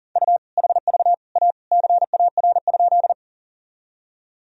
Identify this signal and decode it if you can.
Morse code, heard as a keyed tone, about 37 wpm, 700 Hz. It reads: U HV A CAR?
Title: Sentences from Top 300 Words and Abbreviations 40wpm